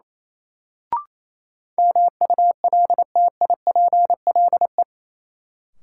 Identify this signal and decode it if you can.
Morse code, heard as a keyed tone, about 28 wpm, 700 Hz. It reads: MULTIPLE
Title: Sets of 5 Words - New Word Drill - 28wpm